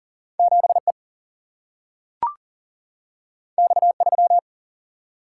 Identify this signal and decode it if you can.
Morse code, heard as a keyed tone, about 40 wpm, 700 Hz. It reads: 7 E X 3
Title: Single-Letter Single-Number 40wpm